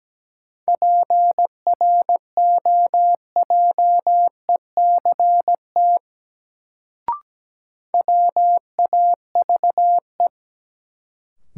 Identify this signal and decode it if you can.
Morse code, heard as a keyed tone, about 17 wpm, 700 Hz. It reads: PROJECT WAVE
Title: Sentences from Top 1400 Words - New Word Drill - 17wpm